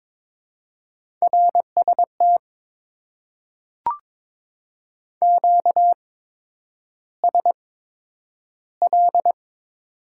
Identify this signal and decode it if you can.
Morse code, heard as a keyed tone, about 22 wpm, spaced as for 9 wpm, 700 Hz. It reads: RST QSL
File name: Top 90 QSO Elements - 8x Character Spacing - 22wpm